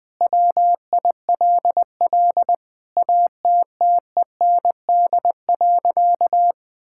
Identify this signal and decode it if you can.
Morse code, heard as a keyed tone, about 20 wpm, 700 Hz. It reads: WILL ATTEND.
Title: Sentences with 7-Word Phrases - 20wpm